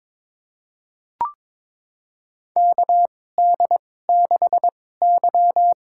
Morse code, heard as a keyed tone, about 21 wpm, spaced as for 16 wpm, 700 Hz. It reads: KD6Y
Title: USA Call Signs - 2x Character Spacing - 22wpm